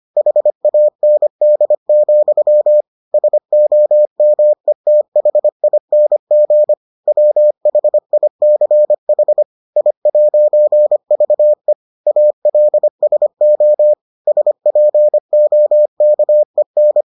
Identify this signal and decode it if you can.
Morse code, heard as a keyed tone, about 25 wpm, 600 Hz. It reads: HAND, SOMETHING WHICH I'VE ALSO SPOKEN